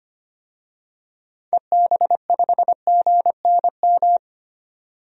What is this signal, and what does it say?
Morse code, heard as a keyed tone, about 25 wpm, 700 Hz. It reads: EB5GNM
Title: DXCC-100 Call Signs - 6 Characters - 25wpm